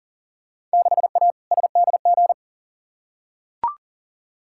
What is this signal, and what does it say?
Morse code, heard as a keyed tone, about 40 wpm, 700 Hz. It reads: 6A SDG
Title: Field Day Exchanges Encore - 40wpm